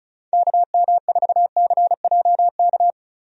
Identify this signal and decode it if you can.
Morse code, heard as a keyed tone, about 36 wpm, 700 Hz. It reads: KM4CJK
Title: USA Call Signs - 6 Characters - Repeated 3x - 35wpm